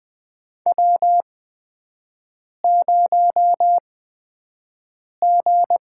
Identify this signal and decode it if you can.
Morse code, heard as a keyed tone, about 20 wpm, spaced as for 5 wpm, 700 Hz. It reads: W0G
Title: USA Call Signs - 4 Characters - 8x Character Spacing - 20wpm